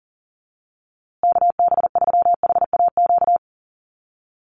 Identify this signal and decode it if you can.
Morse code, heard as a keyed tone, about 40 wpm, 700 Hz. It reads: KB3HAQ
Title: USA Call Signs Repeated 2x 40wpm